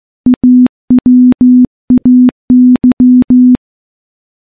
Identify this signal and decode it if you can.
Morse code, heard as a keyed tone, about 15 wpm, 250 Hz. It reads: AWAY
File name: morse / morse-training-15WPM